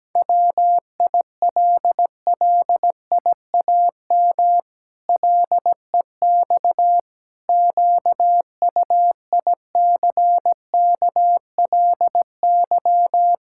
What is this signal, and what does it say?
Morse code, heard as a keyed tone, about 17 wpm, 700 Hz. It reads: WILLIAM LEX QUICKLY
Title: Pangram Challenge v2 - 17wpm